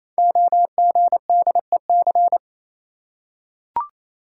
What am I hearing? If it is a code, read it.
Morse code, heard as a keyed tone, about 28 wpm, 700 Hz. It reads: OGDEC
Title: Code Groups - 5 Letters - 28wpm